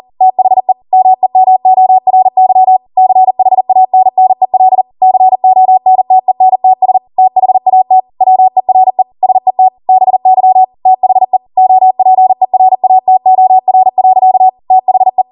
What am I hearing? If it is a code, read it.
Morse code, heard as a keyed tone, about 40 wpm, 750 Hz. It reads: THE MEMORY CHANNEL CONTENTS THAT WERE SET BY THE OPERATOR. THE
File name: Morse / 40WPM